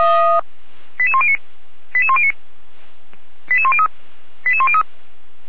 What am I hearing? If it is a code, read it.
Five-tone selective call (ZVEI), heard as quick runs of five tones, tones of about 70 ms. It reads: 88109 88109 88103 88103